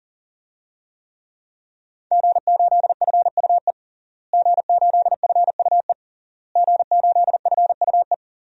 Morse code, heard as a keyed tone, about 40 wpm, 700 Hz. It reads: G8FUE G8FUE G8FUE
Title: DXCC-100 Call Signs - 5 Characters - Repeated 3x - 40wpm